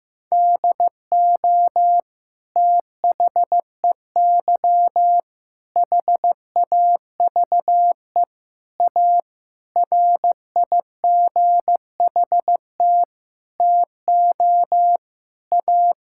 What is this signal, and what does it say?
Morse code, heard as a keyed tone, about 16 wpm, 700 Hz. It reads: DO THEY HAVE A RIGHT TO A
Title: Sentences from Top 1000 Words - Review Part 6 - 15wpm